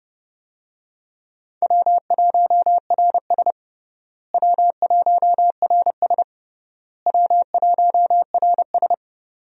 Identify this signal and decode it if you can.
Morse code, heard as a keyed tone, about 30 wpm, 700 Hz. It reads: W1RS W1RS W1RS
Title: USA Call Signs - 4 Characters - Repeated 3x - 30wpm